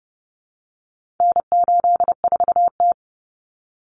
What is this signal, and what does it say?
Morse code, heard as a keyed tone, about 30 wpm, 700 Hz. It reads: N84T
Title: Common Cut Numbers - 4-Digit 30wpm